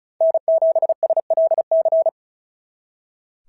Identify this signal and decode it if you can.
Morse code, heard as a keyed tone, about 36 wpm, 690 Hz. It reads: N7SLC
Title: SOTA QSO Practice 35wpm